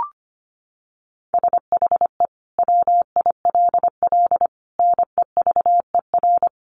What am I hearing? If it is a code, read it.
Morse code, heard as a keyed tone, about 25 wpm, 700 Hz. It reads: SHE WILL NEVER
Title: Sentences from Top 1300 Words 25wpm